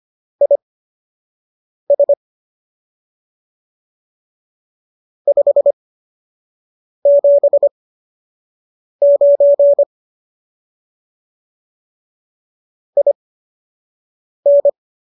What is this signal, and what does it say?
Morse code, heard as a keyed tone, about 25 wpm, spaced as for 6 wpm, 570 Hz. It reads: IS 579 IN